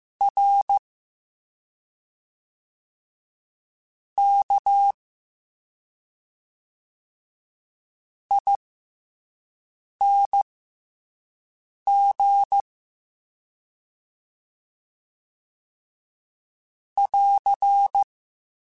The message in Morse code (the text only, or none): RKING <AR>